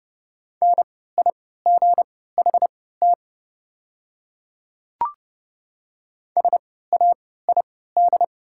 Morse code, heard as a keyed tone, about 30 wpm, spaced as for 17 wpm, 700 Hz. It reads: NIGHT SAID
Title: 3x Character Spacing - Top 300 Words - 30wpm